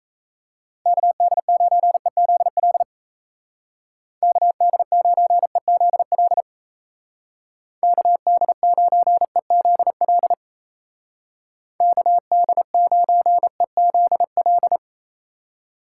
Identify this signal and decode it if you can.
Morse code, heard as a keyed tone, about 35 wpm, 700 Hz. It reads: KD9EZL KD9EZL KD9EZL KD9EZL
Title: Speed Racing - USA 2x3 Call Signs 28wpm